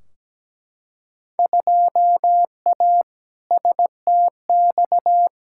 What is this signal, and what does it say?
Morse code, heard as a keyed tone, about 17 wpm, 700 Hz. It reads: 2A STX
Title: Field Day Exchanges - 17wpm